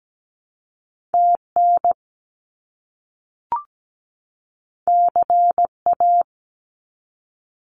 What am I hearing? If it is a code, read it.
Morse code, heard as a keyed tone, about 17 wpm, 700 Hz. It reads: TN CA
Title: US States 17wpm